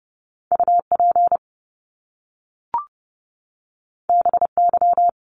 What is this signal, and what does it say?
Morse code, heard as a keyed tone, about 30 wpm, 700 Hz. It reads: UP BY
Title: Top Two Letter Combinations in Words 30wpm